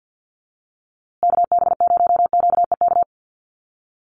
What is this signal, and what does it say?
Morse code, heard as a keyed tone, about 50 wpm, 700 Hz. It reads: KB0QEK
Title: USA Call Signs Repeated 2x 50wpm